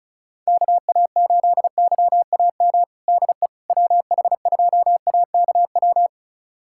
Morse code, heard as a keyed tone, about 36 wpm, 700 Hz. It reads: KA8YAM DE WH2AKW
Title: USA Callsign DE Callsign - 6 Characters - 35wpm